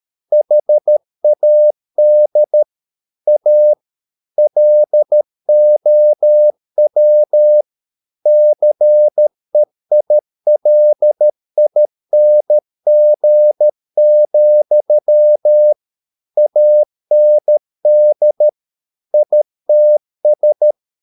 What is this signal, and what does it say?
Morse code, heard as a keyed tone, about 13 wpm, 600 Hz. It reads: HAD A LOW CEILING, AND ITS
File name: Never 0013wpm